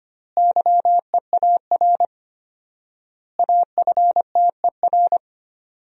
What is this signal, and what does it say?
Morse code, heard as a keyed tone, about 25 wpm, 700 Hz. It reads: YEAR AFTER
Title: 4x Word-Spacing - Sentences from Top 300 Words - 25wpm